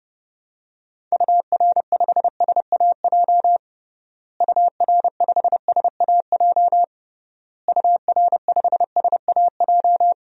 Morse code, heard as a keyed tone, about 32 wpm, 700 Hz. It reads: UR5SAJ UR5SAJ UR5SAJ
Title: DXCC-100 Call Signs - 6 Characters - Repeated 3x - 30wpm